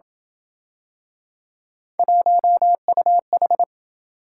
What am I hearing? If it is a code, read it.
Morse code, heard as a keyed tone, about 26 wpm, 700 Hz. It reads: 1UH